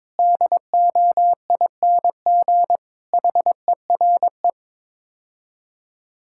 Morse code, heard as a keyed tone, about 22 wpm, 700 Hz. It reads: DOING HERE
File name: practice / 22wpm / Build A Phrase - 5-Word Phrases - 22wpm